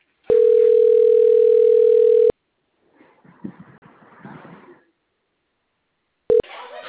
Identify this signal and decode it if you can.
Ringback tone (North America), 440 + 480 Hz.